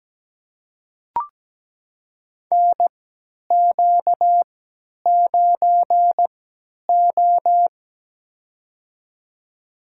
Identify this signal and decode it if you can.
Morse code, heard as a keyed tone, about 17 wpm, spaced as for 10 wpm, 700 Hz. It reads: NQ9O
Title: USA Call Signs - 3x Character Spacing - 17wpm